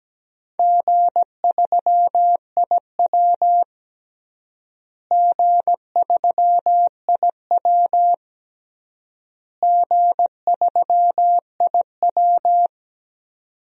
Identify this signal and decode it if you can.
Morse code, heard as a keyed tone, about 17 wpm, 700 Hz. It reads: G3IW G3IW G3IW
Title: DXCC-100 Call Signs - 4 Characters - Repeated 3x - 17wpm